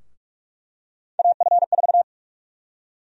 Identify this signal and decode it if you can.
Morse code, heard as a keyed tone, about 46 wpm, 700 Hz. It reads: AF4
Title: USA Call Sign Prefixes - Repeated 2x - 45wpm